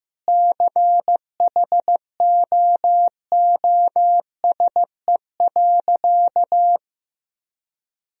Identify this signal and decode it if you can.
Morse code, heard as a keyed tone, about 15 wpm, 700 Hz. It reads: CHOOSE.
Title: Sentences from Top 1000 Words - Review Part 4 - 15wpm